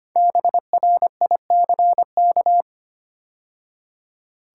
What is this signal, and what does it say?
Morse code, heard as a keyed tone, about 25 wpm, 700 Hz. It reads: BRICK
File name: 5 Letter Words - First 3 Letters Given - 25wpm